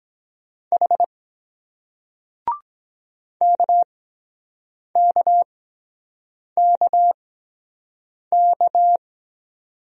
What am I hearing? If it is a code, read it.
Morse code, heard as a keyed tone, about 24 wpm, spaced as for 7 wpm, 700 Hz. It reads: H KKKK